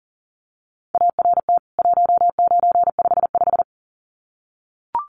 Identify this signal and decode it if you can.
Morse code, heard as a keyed tone, about 40 wpm, 700 Hz. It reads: ART 1955
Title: CWT Exchange 40wpm